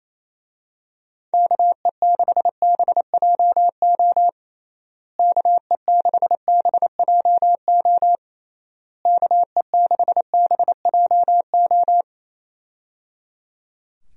Morse code, heard as a keyed tone, about 28 wpm, 700 Hz. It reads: KE6BJO KE6BJO KE6BJO
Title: USA Call Signs - 6 Characters - Repeated 3x - 28wpm